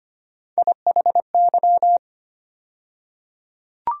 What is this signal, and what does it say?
Morse code, heard as a keyed tone, about 25 wpm, 700 Hz. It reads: IHY
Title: USA Call Sign Postfixes - 3 Letters 25wpm